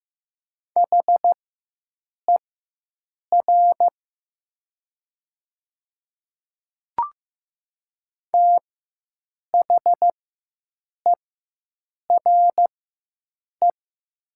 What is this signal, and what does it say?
Morse code, heard as a keyed tone, about 15 wpm, spaced as for 7 wpm, 700 Hz. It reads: HER THERE
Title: 4x Character Spacing - Top 100 Words - 15wpm